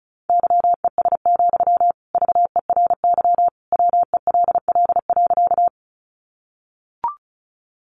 Morse code, heard as a keyed tone, about 34 wpm, 700 Hz. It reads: YES, VERY WELL.
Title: Sentences from Top 500 Words 35wpm